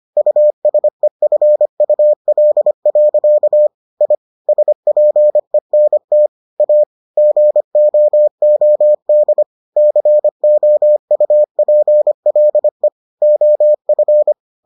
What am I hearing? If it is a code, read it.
Morse code, heard as a keyed tone, about 25 wpm, 600 Hz. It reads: USEFUL. I SPENT A GOOD COUPLE OF